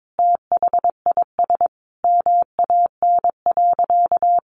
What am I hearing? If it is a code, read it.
Morse code, heard as a keyed tone, about 22 wpm, 700 Hz. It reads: THIS MAN.
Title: Sentences from Top 1400 Words 22wpm